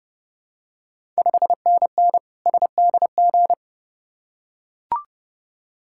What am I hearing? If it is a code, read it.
Morse code, heard as a keyed tone, about 30 wpm, 700 Hz. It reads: 5NN SDG
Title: Ontario QSO Party Exchange 30wpm